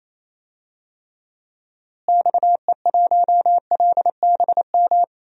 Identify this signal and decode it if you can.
Morse code, heard as a keyed tone, about 28 wpm, 700 Hz. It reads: XE1LBM
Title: DXCC-100 Call Signs - 6 Characters - Repeated 2x - 28wpm